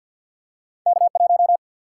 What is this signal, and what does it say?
Morse code, heard as a keyed tone, about 51 wpm, 700 Hz. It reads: K1
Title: Single-Letter Single-Number 50wpm